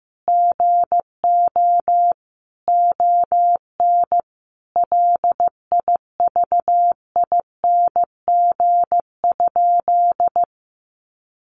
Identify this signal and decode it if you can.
Morse code, heard as a keyed tone, about 15 wpm, 700 Hz. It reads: GO ON LIVING?
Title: Sentences from Top 1100 Words 15wpm